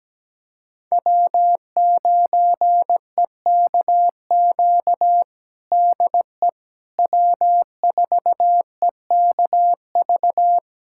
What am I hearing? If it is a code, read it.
Morse code, heard as a keyed tone, about 17 wpm, 700 Hz. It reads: W9EKQ DE W4EKV
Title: USA Callsign DE Callsign - 5 Characters - 17wpm